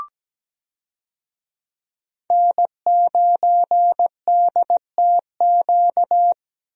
Morse code, heard as a keyed tone, about 17 wpm, 700 Hz. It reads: N9DTQ
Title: USA Call Signs - 5 Characters - Repeated 3x - 17wpm